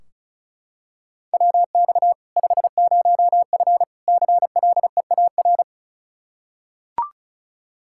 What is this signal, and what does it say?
Morse code, heard as a keyed tone, about 34 wpm, 700 Hz. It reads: WX 50F CLEAR
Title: K1USN SST QSO Practice 35wpm